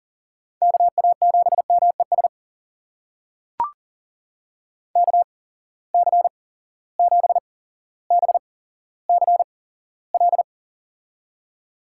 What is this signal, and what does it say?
Morse code, heard as a keyed tone, about 40 wpm, spaced as for 17 wpm, 700 Hz. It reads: KA7MES KC7BCL
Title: USA Call Signs - 6 Characters - 8x Character Spacing - 40wpm